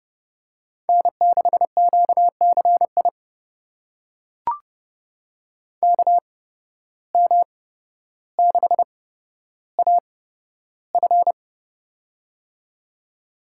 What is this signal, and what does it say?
Morse code, heard as a keyed tone, about 30 wpm, spaced as for 13 wpm, 700 Hz. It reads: N6QCI KM6AF
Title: USA Call Signs - 5 Characters - 8x Character Spacing - 30wpm